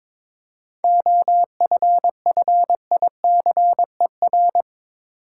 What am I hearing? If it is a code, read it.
Morse code, heard as a keyed tone, about 22 wpm, 700 Hz. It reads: OFFICER
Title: Seven Letter Words - 22wpm